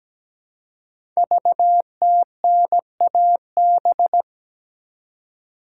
Morse code, heard as a keyed tone, about 17 wpm, 700 Hz. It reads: VTNAB
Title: Code Groups - 5 Letters - 17wpm